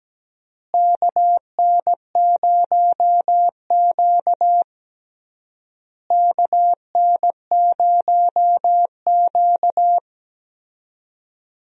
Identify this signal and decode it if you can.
Morse code, heard as a keyed tone, about 17 wpm, 700 Hz. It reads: KN0Q KN0Q
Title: USA Call Signs - 4 Characters - Repeated 3x - 17wpm